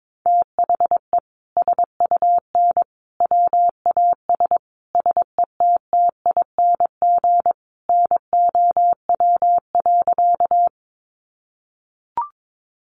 Morse code, heard as a keyed tone, about 22 wpm, 700 Hz. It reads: THE SUN WAS SETTING NOW.
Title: Sentences from Top 1300 Words 22wpm